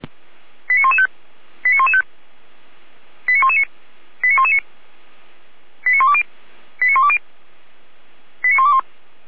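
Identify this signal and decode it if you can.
Five-tone selective call (ZVEI), heard as quick runs of five tones, tones of about 70 ms. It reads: 89115 89115 89119 89119 89120 89120 89121
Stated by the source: ZVEI five-tone call 89072 received by SDR.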